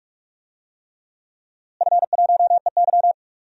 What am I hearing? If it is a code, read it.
Morse code, heard as a keyed tone, about 41 wpm, 700 Hz. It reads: F1EY
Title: DXCC-100 Call Signs - 4 Characters - Repeated 3x - 45wpm